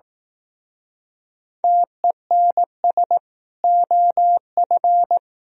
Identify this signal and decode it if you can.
Morse code, heard as a keyed tone, about 18 wpm, 700 Hz. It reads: TENS OF
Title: Speed-Racing - Sets of 4 Words 15wpm